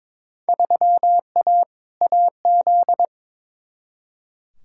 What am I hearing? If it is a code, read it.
Morse code, heard as a keyed tone, about 22 wpm, 700 Hz. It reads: 3A AZ
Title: Field Day Exchanges Encore - 22wpm